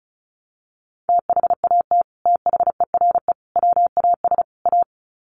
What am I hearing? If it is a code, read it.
Morse code, heard as a keyed tone, about 34 wpm, 700 Hz. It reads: THAT THERE WAS A
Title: Sets of 4 Words 35wpm